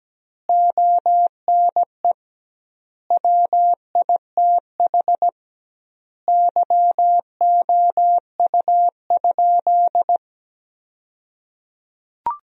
Morse code, heard as a keyed tone, about 17 wpm, 700 Hz. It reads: ONE WITH YOU?
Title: 2x Word-Spacing - Sentences from Top 500 Words - 17wpm